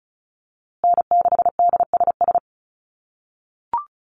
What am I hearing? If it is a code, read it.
Morse code, heard as a keyed tone, about 34 wpm, 700 Hz. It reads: N6DSS
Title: USA Call Signs Repeated 2x 35wpm